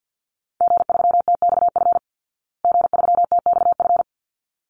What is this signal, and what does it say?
Morse code, heard as a keyed tone, about 50 wpm, 700 Hz. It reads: G3TXF G3TXF
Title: Call Signs Repeated 3x - 50wpm